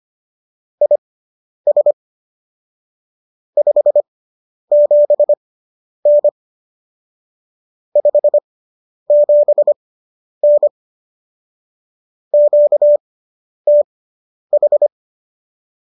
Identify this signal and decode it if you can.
Morse code, heard as a keyed tone, about 25 wpm, spaced as for 10 wpm, 590 Hz. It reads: IS 57N 57N QTH
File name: qso219 10